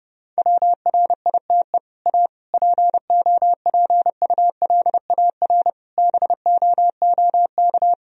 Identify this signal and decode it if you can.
Morse code, heard as a keyed tone, about 30 wpm, 700 Hz. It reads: WRITE A POPULAR BOOK
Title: Sentences from Top 1000 Words - Review Part 3 - 30wpm